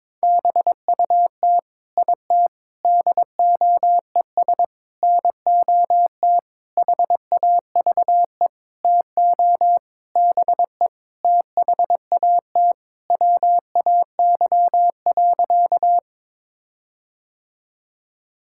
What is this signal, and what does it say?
Morse code, heard as a keyed tone, about 22 wpm, 700 Hz. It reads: BUT IT DOES NOT HAVE TO BE THAT WAY.
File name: practice / 22wpm / Sentences with 8-Word Phrases - 22wpm